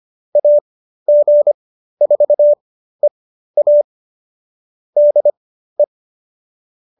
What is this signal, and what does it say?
Morse code, heard as a keyed tone, about 25 wpm, spaced as for 13 wpm, 570 Hz. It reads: AG4EA DE